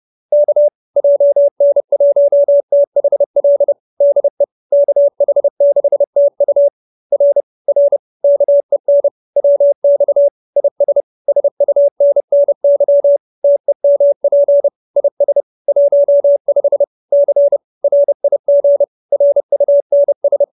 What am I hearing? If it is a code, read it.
Morse code, heard as a keyed tone, about 30 wpm, 570 Hz. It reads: K JN1THL DE KH6TU R R KEN WX IS SUNNY TEMP IS 15 C RIG RUNS
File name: QSO309 30